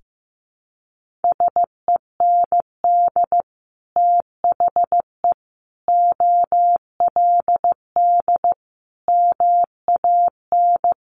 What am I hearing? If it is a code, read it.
Morse code, heard as a keyed tone, about 15 wpm, 700 Hz. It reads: SEND THE OLD MAN
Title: Sentences from Top 1500 Words 15wpm